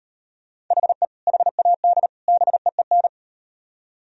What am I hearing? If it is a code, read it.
Morse code, heard as a keyed tone, about 38 wpm, 700 Hz. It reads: HE HAD BEEN